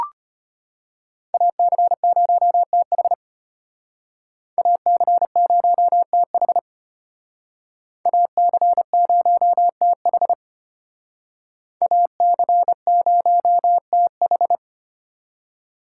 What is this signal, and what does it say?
Morse code, heard as a keyed tone, about 33 wpm, 700 Hz. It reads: AC0TH AC0TH AC0TH AC0TH